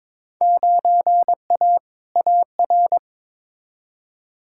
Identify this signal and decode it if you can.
Morse code, heard as a keyed tone, about 22 wpm, 700 Hz. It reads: 9A AR
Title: Field Day Exchanges Encore - 22wpm